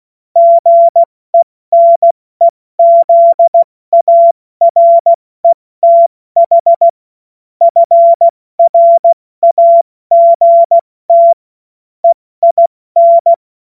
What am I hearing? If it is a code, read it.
Morse code, heard as a keyed tone, about 16 wpm, 680 Hz. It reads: GENEZARETH FRAGT EIN
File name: Morsekurs / Witze / Witz 21. Februar 2023 70 F80